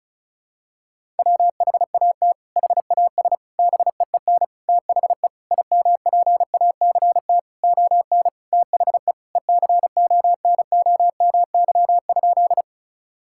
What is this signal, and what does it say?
Morse code, heard as a keyed tone, about 36 wpm, 700 Hz. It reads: WHAT HAS BEEN THE IMPACT ON THE ECONOMY?
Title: Sentences from Top 1000 Words - Review Part 6 - 35wpm